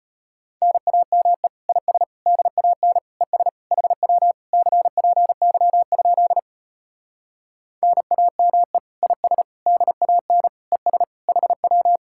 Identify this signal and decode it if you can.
Morse code, heard as a keyed tone, about 33 wpm, 700 Hz. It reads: NAME IS DAN ES HW CPY? NAME IS DAN ES HW